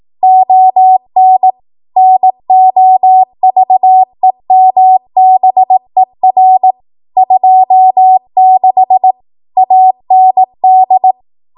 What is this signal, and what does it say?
Morse code, heard as a keyed tone, about 18 wpm, 750 Hz. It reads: ON NOVEMBER 26 AND
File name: Archive / 18 WPM / 18WPM